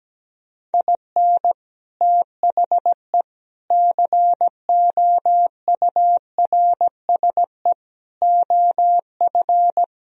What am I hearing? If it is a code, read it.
Morse code, heard as a keyed tone, about 17 wpm, 700 Hz. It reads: IN THE COURSE OF
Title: Speed-Racing - Sets of 5 Words 17wpm